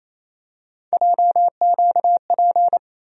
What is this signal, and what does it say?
Morse code, heard as a keyed tone, about 28 wpm, 700 Hz. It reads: JQP
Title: USA Call Sign Postfixes - 3 Letters 28wpm